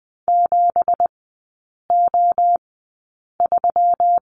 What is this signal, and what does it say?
Morse code, heard as a keyed tone, about 20 wpm, spaced as for 8 wpm, 690 Hz. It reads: 7O3
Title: 166 Lesson 27e - Number 7 - Three Characters 20wpm